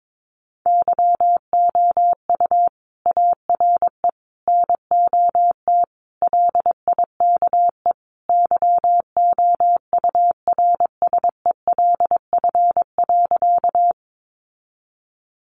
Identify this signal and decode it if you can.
Morse code, heard as a keyed tone, about 22 wpm, 700 Hz. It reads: YOU ARE NOT LIKE YOURSELF.
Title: Sentences from Top 600 Words 22wpm